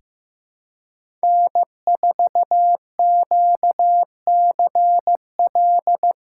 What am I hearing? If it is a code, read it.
Morse code, heard as a keyed tone, about 15 wpm, 700 Hz. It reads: N4QCL
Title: USA Callsign DE Callsign - 5 Characters - 15wpm